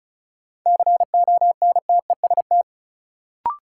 Morse code, heard as a keyed tone, about 36 wpm, 700 Hz. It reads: CONTEST
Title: Sentences from Top 1600 Words - New Word Drill - 35wpm